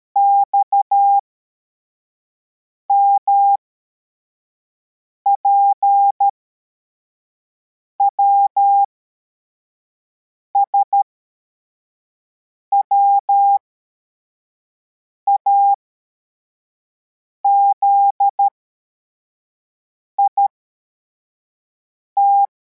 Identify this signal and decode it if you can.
Morse code, heard as a keyed tone, about 13 wpm, spaced as for 4 wpm, 800 Hz. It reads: XMPWS WAZIT
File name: formation / cw / Text85